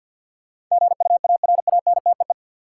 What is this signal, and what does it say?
Morse code, heard as a keyed tone, about 52 wpm, 700 Hz. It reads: GUARANTEE
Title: Sentences from Top 1600 Words - New Word Drill - 50wpm